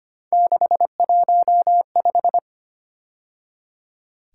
Morse code, heard as a keyed tone, about 29 wpm, 700 Hz. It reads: 615
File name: Speed-Racing - Three Letter-Number Combinations - 25wpm